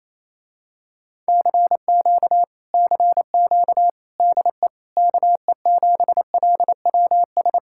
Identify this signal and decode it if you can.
Morse code, heard as a keyed tone, about 26 wpm, 700 Hz. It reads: CQ CQ DE KE7LWS
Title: Speed-Racing Common QSO Phrases - 28wpm